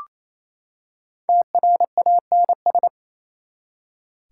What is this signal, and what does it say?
Morse code, heard as a keyed tone, about 28 wpm, 700 Hz. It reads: TRANS
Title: Prefixes and Suffixes 28wpm